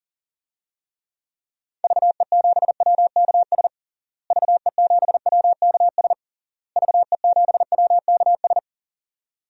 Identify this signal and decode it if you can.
Morse code, heard as a keyed tone, about 37 wpm, 700 Hz. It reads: VE7WKS VE7WKS VE7WKS
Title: DXCC-100 Call Signs - 6 Characters - Repeated 3x - 40wpm